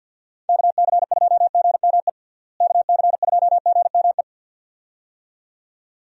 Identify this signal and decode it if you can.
Morse code, heard as a keyed tone, about 50 wpm, 700 Hz. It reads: KC2GME KC2GME
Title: USA Call Signs - 6 Characters - Repeated 2x - 50wpm